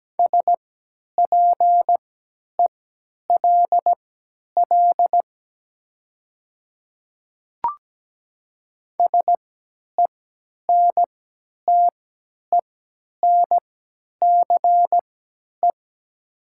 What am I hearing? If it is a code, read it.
Morse code, heard as a keyed tone, about 17 wpm, spaced as for 10 wpm, 700 Hz. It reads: SPELL SENTENCE
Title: 3x Character Spacing - Top 200 Words - 17wpm